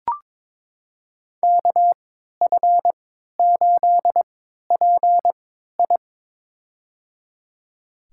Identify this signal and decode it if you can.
Morse code, heard as a keyed tone, about 22 wpm, spaced as for 12 wpm, 700 Hz. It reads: KF8PI
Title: USA Call Signs - 5 Characters - 3x Character Spacing - 22wpm